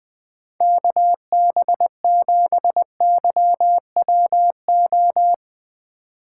Morse code, heard as a keyed tone, about 21 wpm, 700 Hz. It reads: KB7YWO
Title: USA Call Signs - 6 Characters - 2x Character Spacing - 20wpm